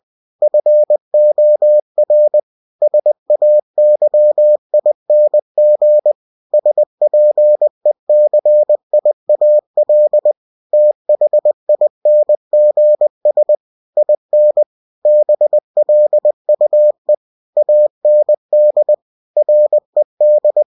Morse code, heard as a keyed tone, about 20 wpm, 600 Hz. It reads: FOR SAYING SPECIAL THINGS IN BLUE AND RED
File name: cw-books / winnie-the-pooh / winnie-20wpm-page-0109